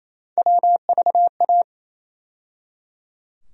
Morse code, heard as a keyed tone, about 27 wpm, 700 Hz. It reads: WVA
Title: USA Call Sign Postfixes - 3 Letters 28wpm